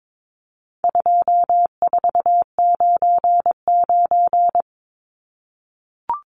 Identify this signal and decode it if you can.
Morse code, heard as a keyed tone, about 22 wpm, 700 Hz. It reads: 2499
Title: Numbers - 4-Digit 22wpm